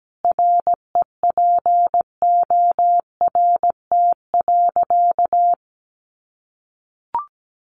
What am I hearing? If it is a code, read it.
Morse code, heard as a keyed tone, about 17 wpm, 700 Hz. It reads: REPORT.
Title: Sentences from Top 1300 Words 17wpm